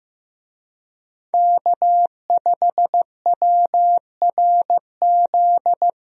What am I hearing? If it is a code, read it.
Morse code, heard as a keyed tone, about 15 wpm, 700 Hz. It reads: K5WRZ
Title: USA Call Signs - 5 Characters - 15wpm